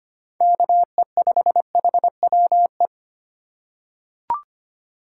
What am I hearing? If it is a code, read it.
Morse code, heard as a keyed tone, about 25 wpm, 700 Hz. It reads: KE5HWE
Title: USA Call Signs - 6 Characters - 2x Character Spacing - 25wpm